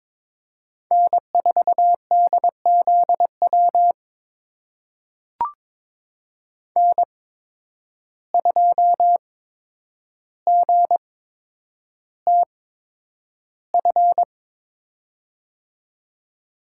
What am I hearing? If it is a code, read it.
Morse code, heard as a keyed tone, about 22 wpm, spaced as for 9 wpm, 700 Hz. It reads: N4DZW N2GTF
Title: USA Call Signs - 5 Characters - 8x Character Spacing - 22wpm